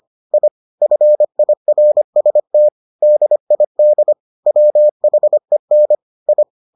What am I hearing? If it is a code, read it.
Morse code, heard as a keyed tone, about 25 wpm, 600 Hz. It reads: I FIRST DID WHEN I